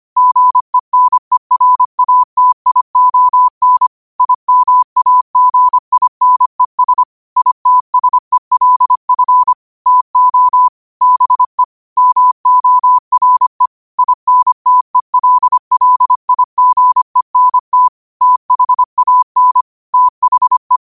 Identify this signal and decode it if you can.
Morse code, heard as a keyed tone, about 25 wpm, 1000 Hz. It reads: GENERATION IMAGINES ITSELF TO BE MORE INTELLIGENT THAN THE
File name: qotdfiles / 25wpm